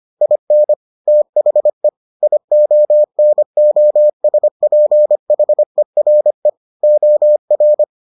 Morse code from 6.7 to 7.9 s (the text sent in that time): OR